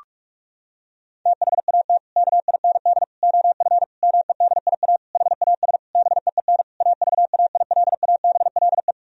Morse code, heard as a keyed tone, about 41 wpm, 700 Hz. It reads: THAT KIND OF MEDIA HAS BEEN AVAILABLE
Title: Sentences from Top 1000 Words - Review Part 4 - 45wpm